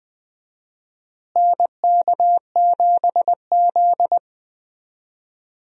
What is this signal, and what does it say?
Morse code, heard as a keyed tone, about 21 wpm, 700 Hz. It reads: NK7Z
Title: USA Call Signs - 4 Characters - Repeated 3x - 20wpm